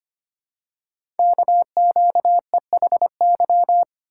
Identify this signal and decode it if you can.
Morse code, heard as a keyed tone, about 25 wpm, 690 Hz. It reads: KQEHY
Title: Code Groups - 5 Letters - 25wpm